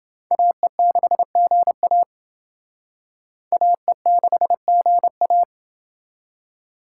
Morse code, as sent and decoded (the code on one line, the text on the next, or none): .- . -.... --. .- / .- . -.... --. .-
AE6GA AE6GA